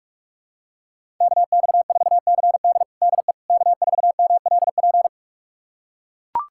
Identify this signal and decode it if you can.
Morse code, heard as a keyed tone, about 44 wpm, 700 Hz. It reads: KX4CD DE K4MLP
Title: USA Callsign DE Callsign - 5 Characters - 45wpm